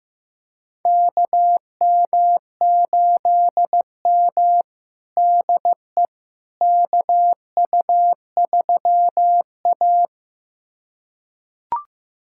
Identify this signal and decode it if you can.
Morse code, heard as a keyed tone, about 15 wpm, 700 Hz. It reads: KM8M DE KU3A
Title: USA Callsign DE Callsign - 4 Characters - 15wpm